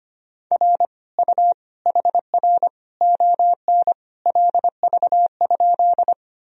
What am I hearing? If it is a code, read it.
Morse code, heard as a keyed tone, about 25 wpm, 700 Hz. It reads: R U HR ON LV?
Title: Sentences from Top 400 Words and Abbreviations 25wpm